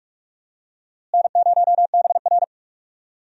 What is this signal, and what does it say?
Morse code, heard as a keyed tone, about 46 wpm, 700 Hz. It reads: N0BR
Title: USA Call Signs - 4 Characters - Repeated 3x - 45wpm